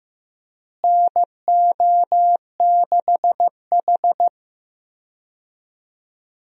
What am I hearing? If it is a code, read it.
Morse code, heard as a keyed tone, about 15 wpm, 700 Hz. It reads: NO6H
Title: USA Call Signs - 4 Characters - Repeated 2x - 15wpm